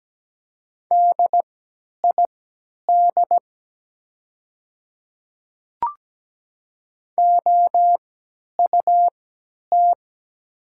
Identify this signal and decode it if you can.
Morse code, heard as a keyed tone, about 17 wpm, spaced as for 10 wpm, 700 Hz. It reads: DID OUT
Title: 3x Character Spacing - Top 100 Words - 17wpm